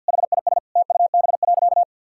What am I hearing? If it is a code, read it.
Morse code, heard as a keyed tone, about 48 wpm, 700 Hz. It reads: HIS TUB.
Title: Pangram Challenge v2 - 50wpm